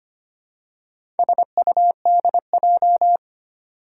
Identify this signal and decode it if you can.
Morse code, heard as a keyed tone, about 25 wpm, 700 Hz. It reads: SUDJ
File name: practice / 25wpm / Four Letter-Number Combinations - 25wpm